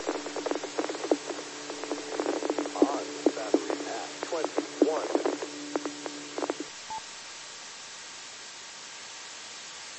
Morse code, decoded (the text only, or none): none